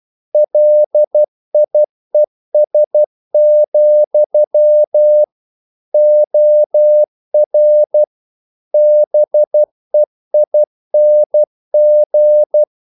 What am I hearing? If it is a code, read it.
Morse code, heard as a keyed tone, about 12 wpm, 600 Hz. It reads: LIES, OR BEING
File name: CW / Practice / If12